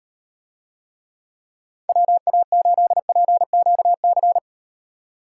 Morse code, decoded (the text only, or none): WA8PQC